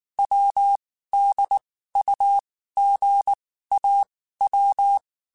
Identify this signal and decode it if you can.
Morse code, heard as a keyed tone, about 19 wpm, 800 Hz. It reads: W D U G A W